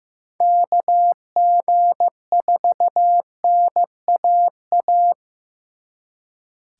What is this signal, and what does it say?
Morse code, heard as a keyed tone, about 18 wpm, 700 Hz. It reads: KG4NAA
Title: Speed Racing - USA 2x3 Call Signs 15wpm